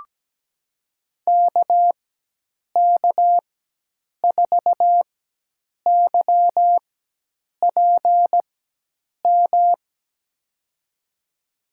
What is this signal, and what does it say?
Morse code, heard as a keyed tone, about 17 wpm, spaced as for 8 wpm, 700 Hz. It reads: KK4YPM
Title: USA Call Signs - 6 Characters - 4x Character Spacing - 17wpm